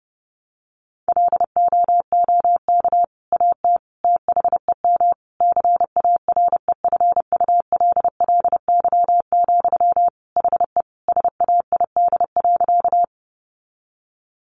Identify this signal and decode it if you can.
Morse code, heard as a keyed tone, about 28 wpm, 700 Hz. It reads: LOOK AT THEM CAREFULLY, HE SAID.
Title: Sentences from Top 1300 Words 30wpm